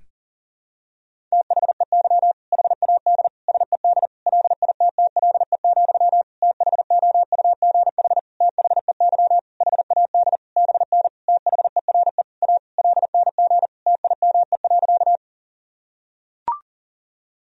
Morse code, heard as a keyed tone, about 37 wpm, 700 Hz. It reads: THEY HAD SED LITTLE, THOUGH THEY HAD BN THERE A LNG TIME.
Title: Sentences from Top 400 Words and Abbreviations 40wpm